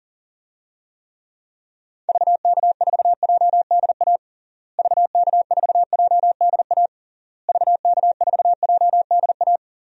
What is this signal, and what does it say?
Morse code, heard as a keyed tone, about 40 wpm, 700 Hz. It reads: VK4JDA VK4JDA VK4JDA
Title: DXCC-100 Call Signs - 6 Characters - Repeated 3x - 40wpm